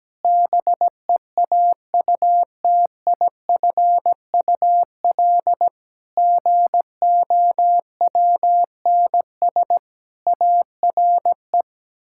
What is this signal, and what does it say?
Morse code, heard as a keyed tone, about 17 wpm, 700 Hz. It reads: BEAUTIFUL GOWNS ARE
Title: Pangram Challenge v2 - 17wpm